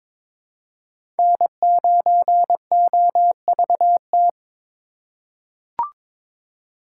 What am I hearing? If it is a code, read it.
Morse code, heard as a keyed tone, about 21 wpm, 700 Hz. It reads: N9OVT
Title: USA Call Signs - 4x Character Spacing - 22wpm